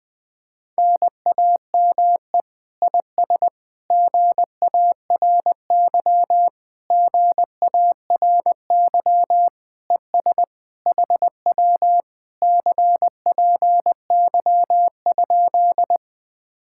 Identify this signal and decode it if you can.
Morse code, heard as a keyed tone, about 20 wpm, 700 Hz. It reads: NAME IS GARY GARY ES HW CPY?